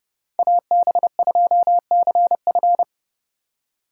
Morse code, heard as a keyed tone, about 32 wpm, 700 Hz. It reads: AB2CF
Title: USA Call Signs - 5 Characters - 2x Character Spacing - 30wpm